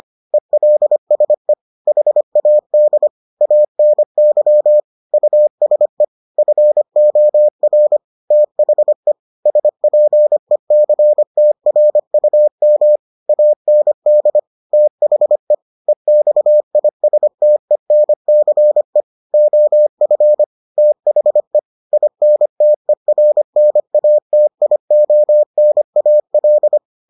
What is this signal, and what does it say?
Morse code, heard as a keyed tone, about 25 wpm, 600 Hz. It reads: ELSE HAD ANY USE FOR THE SPECTRUM AND THE EXISTENCE OF THE INTERNATIONAL